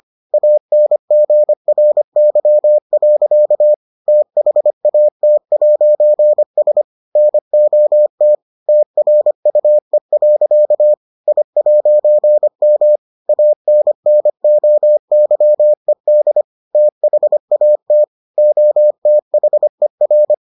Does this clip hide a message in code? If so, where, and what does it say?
Yes: Morse code from 0.3 to 20.3 s, heard as a keyed tone, about 25 wpm, 600 Hz. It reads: ANGRY. THAT'S NOT TRUE. I'M ANNOYED THAT OTHER